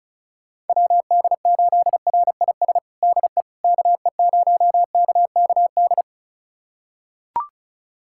Morse code, heard as a keyed tone, about 36 wpm, 700 Hz. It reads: WD8RIS DE KE0KKD
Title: USA Callsign DE Callsign - 6 Characters - 35wpm